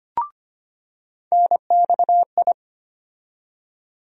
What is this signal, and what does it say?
Morse code, heard as a keyed tone, about 25 wpm, 700 Hz. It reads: NXI
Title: USA Call Sign Postfixes - 3 Letters 25wpm